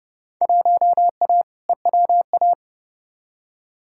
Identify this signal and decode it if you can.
Morse code, heard as a keyed tone, about 30 wpm, 700 Hz. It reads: 1A EWA
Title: Field Day Exchanges Encore - 30wpm